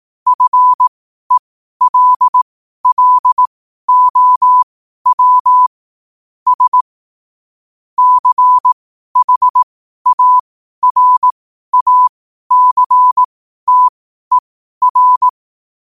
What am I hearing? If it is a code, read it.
Morse code, heard as a keyed tone, about 18 wpm, spaced as for 12 wpm, 1000 Hz. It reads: FELLOW S CHARACTER
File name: qotdfiles / 12wpm